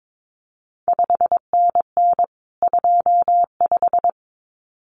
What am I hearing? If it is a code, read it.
Morse code, heard as a keyed tone, about 22 wpm, 700 Hz. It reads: 5NN 25
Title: CQ WW DX Contest Exchange Encore - 22wpm